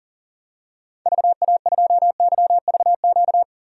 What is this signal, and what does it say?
Morse code, heard as a keyed tone, about 40 wpm, 700 Hz. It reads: VA2YVQ
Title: DXCC-100 Call Signs - 6 Characters - 40wpm